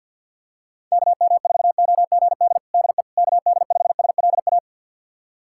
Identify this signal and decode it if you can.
Morse code, heard as a keyed tone, about 47 wpm, 700 Hz. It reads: KM4OGD DE KD5SLA
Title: USA Callsign DE Callsign - 6 Characters - 50wpm